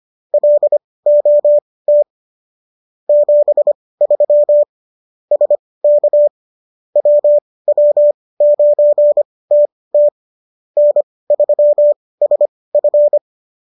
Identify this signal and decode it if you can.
Morse code, heard as a keyed tone, about 25 wpm, spaced as for 18 wpm, 590 Hz. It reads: LOT 73 SK WW9TT N3SF